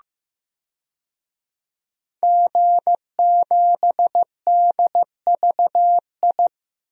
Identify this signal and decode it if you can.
Morse code, heard as a keyed tone, about 15 wpm, 700 Hz. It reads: G7DVI
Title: DXCC-100 Call Signs - 5 Characters - Repeated 3x - 15wpm